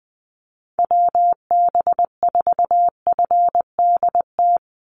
Morse code, heard as a keyed tone, about 20 wpm, 700 Hz. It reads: WB4FDT
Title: Call Signs Repeated 3x - 20wpm